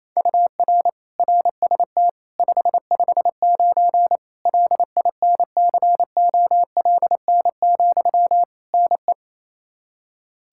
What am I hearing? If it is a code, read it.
Morse code, heard as a keyed tone, about 28 wpm, 700 Hz. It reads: UR RST 559 LINCOLN, NE
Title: Practice QSO - Exchange 1 - Beginning - 28wpm